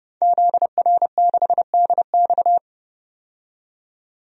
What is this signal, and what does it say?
Morse code, heard as a keyed tone, about 30 wpm, 700 Hz. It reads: ZR6DX
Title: DXCC-100 Call Signs - 5 Characters - 30wpm